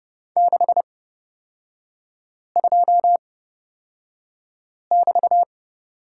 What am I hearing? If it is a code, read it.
Morse code, heard as a keyed tone, about 30 wpm, spaced as for 5 wpm, 700 Hz. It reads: 62<BT>
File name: Lesson 36e - BT - Three Characters 30wpm